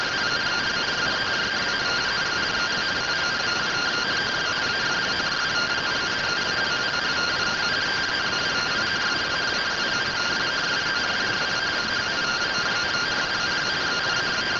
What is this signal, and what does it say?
Two-tone teleprinter signal, about 170 Hz shift.